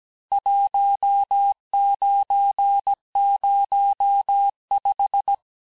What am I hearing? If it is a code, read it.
Morse code, heard as a keyed tone, about 17 wpm, 800 Hz. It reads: 1905
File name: cours-du-lundi files / cours-1587-texte-clair-vitesse-16mpm-puis-21mpm